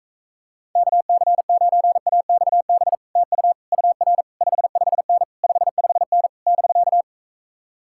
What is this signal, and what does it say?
Morse code, heard as a keyed tone, about 44 wpm, 700 Hz. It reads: KC9AXB TU UR 55N 55N <BK>